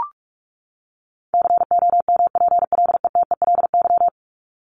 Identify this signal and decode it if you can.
Morse code, heard as a keyed tone, about 44 wpm, 700 Hz. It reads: COMPLETELY
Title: LY Suffix - Top 100 Words with Base Words 45wpm